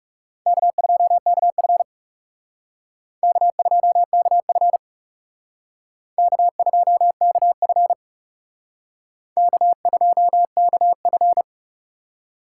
Morse code, heard as a keyed tone, about 39 wpm, 700 Hz. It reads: K2KF K2KF K2KF K2KF